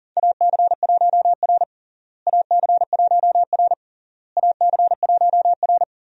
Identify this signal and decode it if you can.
Morse code, heard as a keyed tone, about 37 wpm, 700 Hz. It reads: AC1R AC1R AC1R
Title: USA Call Signs - 4 Characters - Repeated 3x - 40wpm